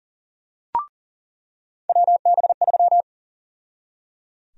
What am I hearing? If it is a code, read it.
Morse code, heard as a keyed tone, about 37 wpm, 700 Hz. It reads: WB3
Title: USA Call Sign Prefixes - 40wpm